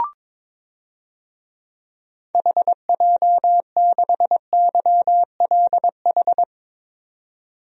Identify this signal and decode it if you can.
Morse code, heard as a keyed tone, about 22 wpm, 700 Hz. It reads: HJ6YLH
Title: DXCC-100 Call Signs - 6 Characters - 22wpm